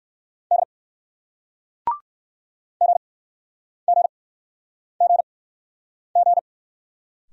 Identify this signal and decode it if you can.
Morse code, heard as a keyed tone, about 61 wpm, 700 Hz. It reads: D G G G G